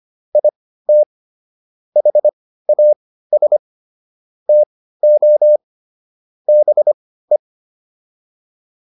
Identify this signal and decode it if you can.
Morse code, heard as a keyed tone, about 25 wpm, spaced as for 15 wpm, 600 Hz. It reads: IT HAS TO BE
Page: intermediate-practice-files